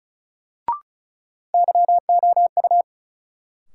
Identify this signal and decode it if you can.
Morse code, heard as a keyed tone, about 36 wpm, 700 Hz. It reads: YOU
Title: Sets of 2 Words - New Word Drill - 35wpm